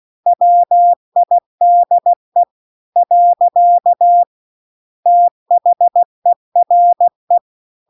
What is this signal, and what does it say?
Morse code, heard as a keyed tone, about 16 wpm, 700 Hz. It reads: WIDE . THERE